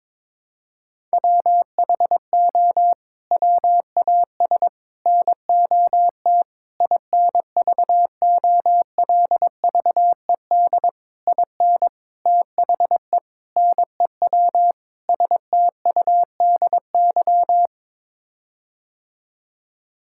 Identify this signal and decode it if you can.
Morse code, heard as a keyed tone, about 22 wpm, 700 Hz. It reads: WHO WAS NOT INVOLVED IN THE NEW STUDY
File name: practice / 22wpm / Sets of 8 Words - 22wpm